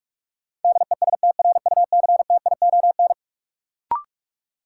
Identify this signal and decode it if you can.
Morse code, heard as a keyed tone, about 46 wpm, 700 Hz. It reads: DESTRUCTION
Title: Sets of 5 Words Encore - New Word Drill - 45wpm